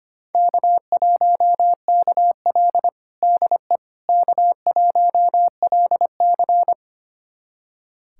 Morse code, heard as a keyed tone, about 25 wpm, 700 Hz. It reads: K1KL DE K1LC
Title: USA Callsign DE Callsign - 4 Characters - 25wpm